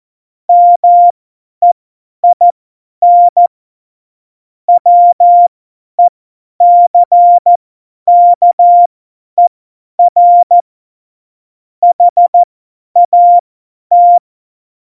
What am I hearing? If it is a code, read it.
Morse code, heard as a keyed tone, about 14 wpm, spaced as for 10 wpm, 700 Hz. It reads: MEIN WECKER HAT
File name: Morsekurs / Witze / Witz 14. Mai 2019, 50 F70